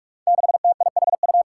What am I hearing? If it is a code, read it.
Morse code, heard as a keyed tone, about 46 wpm, 700 Hz. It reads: 6TIHU
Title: Five Challenging Letter-Number Combinations 45wpm